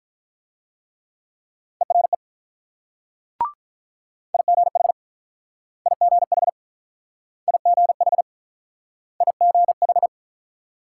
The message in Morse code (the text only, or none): ERE IGH IGH IGH IGH